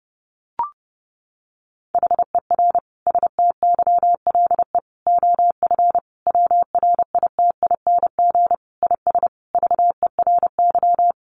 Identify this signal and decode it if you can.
Morse code, heard as a keyed tone, about 30 wpm, 700 Hz. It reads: HER STYLE OF WRITING IS VERY
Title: Sentences from Top 800 Words 30wpm